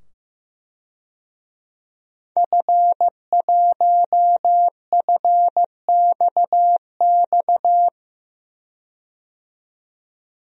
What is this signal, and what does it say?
Morse code, heard as a keyed tone, about 15 wpm, 700 Hz. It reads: F1FXX
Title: DXCC-100 Call Signs - 5 Characters - 15wpm